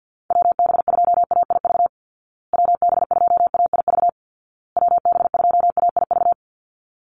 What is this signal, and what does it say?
Morse code, heard as a keyed tone, about 49 wpm, 700 Hz. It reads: WB2AIV WB2AIV WB2AIV
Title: Call Signs Repeated 2x - 50wpm